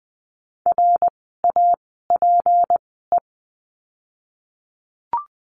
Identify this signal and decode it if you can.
Morse code, heard as a keyed tone, about 20 wpm, spaced as for 15 wpm, 700 Hz. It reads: RAPE
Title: 153 Lesson 25f - Letter p - Words 20wpm